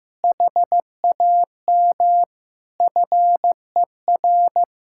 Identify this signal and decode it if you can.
Morse code, heard as a keyed tone, about 15 wpm, 700 Hz. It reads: HAM FER
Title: Practice QSO - Exchange 3 - More Info - 15wpm